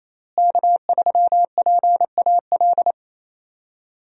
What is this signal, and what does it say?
Morse code, heard as a keyed tone, about 28 wpm, 700 Hz. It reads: K3PAL
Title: USA Call Signs - 5 Characters - Repeated 2x - 28wpm